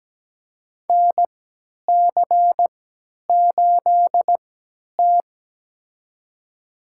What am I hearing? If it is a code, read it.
Morse code, heard as a keyed tone, about 17 wpm, spaced as for 10 wpm, 700 Hz. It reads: NC8T
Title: USA Call Signs - 4 Characters - 3x Character Spacing - 17wpm